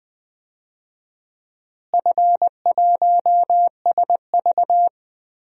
Morse code, heard as a keyed tone, about 20 wpm, 700 Hz. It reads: F1SV
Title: DXCC-100 Call Signs - 4 Characters - Repeated 3x - 20wpm